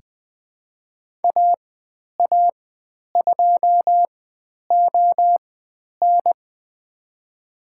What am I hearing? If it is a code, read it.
Morse code, heard as a keyed tone, about 20 wpm, spaced as for 10 wpm, 700 Hz. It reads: AA2ON
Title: 102 Lesson 17g - Letter u - Call Signs 20wpm